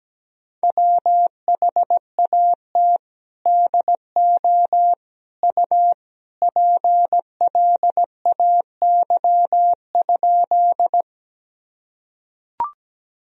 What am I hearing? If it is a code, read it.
Morse code, heard as a keyed tone, about 17 wpm, 700 Hz. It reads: WHAT DO U PLAY?
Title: Sentences from Top 200 Words and Abbreviations 17wpm